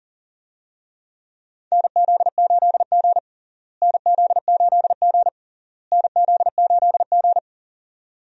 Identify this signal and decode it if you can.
Morse code, heard as a keyed tone, about 37 wpm, 700 Hz. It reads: NZ8G NZ8G NZ8G
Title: USA Call Signs - 4 Characters - Repeated 3x - 40wpm